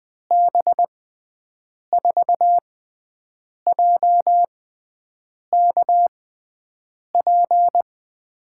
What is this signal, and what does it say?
Morse code, heard as a keyed tone, about 21 wpm, spaced as for 7 wpm, 700 Hz. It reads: B4JKP
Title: USA Call Signs - 6 Characters - 6x Character Spacing - 20wpm